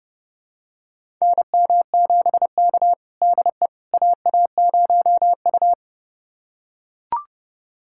Morse code, heard as a keyed tone, about 30 wpm, 700 Hz. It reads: NM7K DE AA0U
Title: USA Callsign DE Callsign - 4 Characters - 30wpm